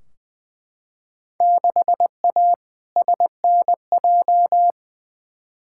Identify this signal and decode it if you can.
Morse code, heard as a keyed tone, about 20 wpm, 700 Hz. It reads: 6A SNJ
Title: Field Day Exchanges Encore - 20wpm